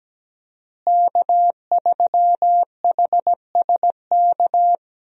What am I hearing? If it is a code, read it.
Morse code, heard as a keyed tone, about 17 wpm, 700 Hz. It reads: K3HSK
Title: USA Call Signs - 5 Characters - Repeated 2x - 17wpm